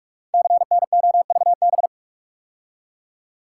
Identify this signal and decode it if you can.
Morse code, heard as a keyed tone, about 44 wpm, 690 Hz. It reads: CNOVB
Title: Code Groups - 5 Letters - 45wpm